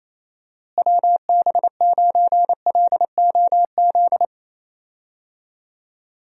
Morse code, heard as a keyed tone, about 27 wpm, 700 Hz. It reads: WB9LOZ
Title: USA Call Signs - 6 Characters - Repeated 2x - 28wpm